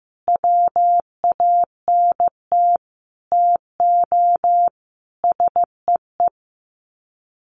Morse code, heard as a keyed tone, about 15 wpm, 700 Hz. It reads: WANT TO SEE
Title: Sets of 5 Words Encore2 15wpm